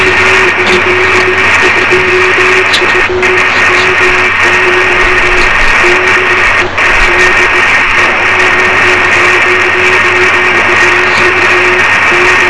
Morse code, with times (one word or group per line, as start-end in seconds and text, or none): none